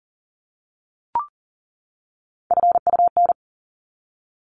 Morse code, heard as a keyed tone, about 40 wpm, 700 Hz. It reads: FUN
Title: Top 501-750 Words 40wpm